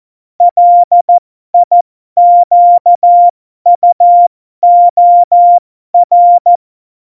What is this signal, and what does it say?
Morse code, heard as a keyed tone, about 14 wpm, spaced as for 12 wpm, 700 Hz. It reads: LIQUOR
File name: Morsekurs / Witze